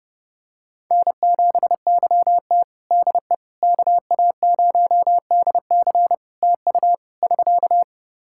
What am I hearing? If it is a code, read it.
Morse code, heard as a keyed tone, about 30 wpm, 700 Hz. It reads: N7YT DE KA0DC TU <SK>
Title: Practice QSO - Exchange 4 - Ending - 30wpm